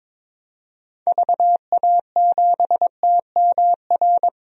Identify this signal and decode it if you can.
Morse code, heard as a keyed tone, about 22 wpm, 700 Hz. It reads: VA7TMR
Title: DXCC-100 Call Signs - 6 Characters - Repeated 3x - 22wpm